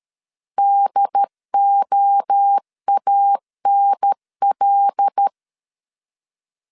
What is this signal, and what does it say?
Morse code, heard as a keyed tone, about 13 wpm, 800 Hz. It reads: DOANL
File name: CW / DET / 1 6 1500C S B 12wpm 3431